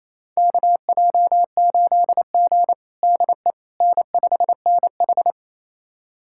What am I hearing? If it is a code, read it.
Morse code, heard as a keyed tone, about 28 wpm, 700 Hz. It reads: KJ8G DE N5NH
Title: USA Callsign DE Callsign - 4 Characters - 28wpm